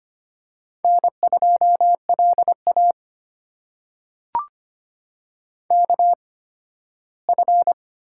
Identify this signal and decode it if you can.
Morse code, heard as a keyed tone, about 25 wpm, spaced as for 15 wpm, 700 Hz. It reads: N2LA KF
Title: USA Call Signs - 8x Character Spacing - 25wpm